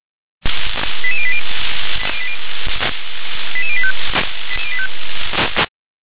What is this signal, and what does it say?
Five-tone selective call (ZVEI), heard as quick runs of five tones, tones of about 70 ms. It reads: 88080 88085